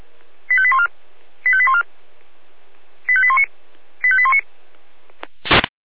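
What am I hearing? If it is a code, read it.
Five-tone selective call (ZVEI), heard as quick runs of five tones, tones of about 70 ms. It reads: 85814 85814 85819 85819
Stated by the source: ZVEI five-tone selective call 85810 received via SDR, logged on the Feuerwehr Orlamünde fire brigade server.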